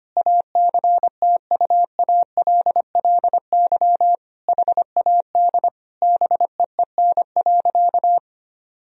Morse code, heard as a keyed tone, about 25 wpm, 690 Hz. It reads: ACTUALLY HAD BEEN.
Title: Sentences from Top 1000 Words - Review Part 4 - 25wpm